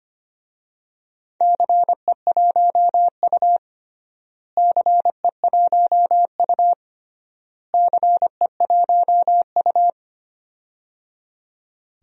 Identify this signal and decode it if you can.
Morse code, heard as a keyed tone, about 25 wpm, 700 Hz. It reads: CE1U CE1U CE1U
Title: DXCC-100 Call Signs - 4 Characters - Repeated 3x - 25wpm